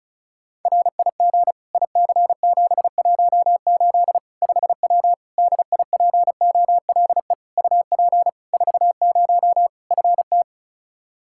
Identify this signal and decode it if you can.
Morse code, heard as a keyed tone, about 36 wpm, 700 Hz. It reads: RIG IC718 5W DIPOLE UP 40 FT
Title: K1USN SST QSO Practice 35wpm